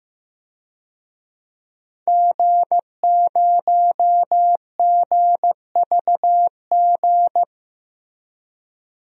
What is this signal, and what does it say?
Morse code, heard as a keyed tone, about 15 wpm, 700 Hz. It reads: G0GVG
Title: DXCC-100 Call Signs - 5 Characters - 15wpm